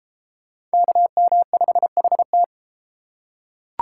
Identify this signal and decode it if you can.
Morse code, heard as a keyed tone, about 32 wpm, 700 Hz. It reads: KM5HT